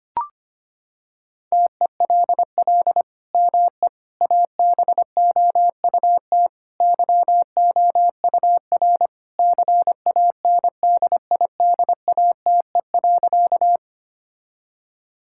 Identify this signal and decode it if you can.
Morse code, heard as a keyed tone, about 25 wpm, 700 Hz. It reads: TELL ME ABOUT YOUR CANDIDATE.
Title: Sentences from Top 1600 Words 25wpm